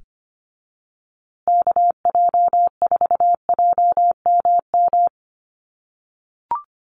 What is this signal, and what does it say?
Morse code, heard as a keyed tone, about 25 wpm, 700 Hz. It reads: KJ4JMM
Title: USA Call Signs Repeated 3x 25wpm